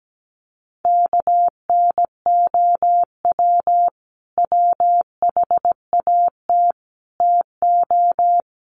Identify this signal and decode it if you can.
Morse code, heard as a keyed tone, about 17 wpm, 700 Hz. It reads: KNOW WHAT TO
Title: Sets of 5 Words 17wpm